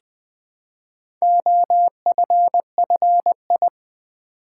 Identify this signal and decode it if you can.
Morse code, heard as a keyed tone, about 20 wpm, 700 Hz. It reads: OFFI
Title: Build A Word - 2 Letters At A Time - 6-Letter Words - 20wpm